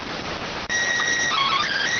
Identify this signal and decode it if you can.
SSTV (slow-scan television), PD 290.